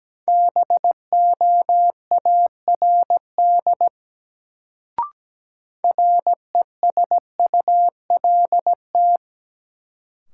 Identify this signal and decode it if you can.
Morse code, heard as a keyed tone, about 17 wpm, 700 Hz. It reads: BOARD RESULT
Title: Sentences from Top 700 Words - New Word Drill - 17wpm